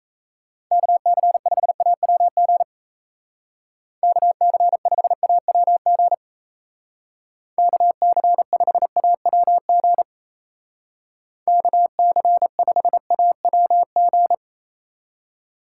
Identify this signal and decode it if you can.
Morse code, heard as a keyed tone, about 35 wpm, 700 Hz. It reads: KC5AWG KC5AWG KC5AWG KC5AWG
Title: Speed Racing - USA Call Signs 28wpm